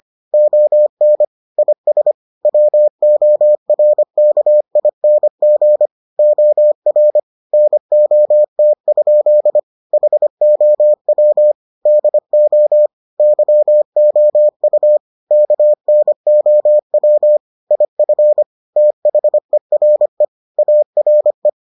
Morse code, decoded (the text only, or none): ON IS WORKING OR NOT? HOW DO YOU KNOW IF THERE ARE